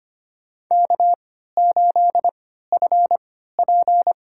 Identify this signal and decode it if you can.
Morse code, heard as a keyed tone, about 25 wpm, spaced as for 14 wpm, 700 Hz. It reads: K8FP
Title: USA Call Signs - 3x Character Spacing - 25wpm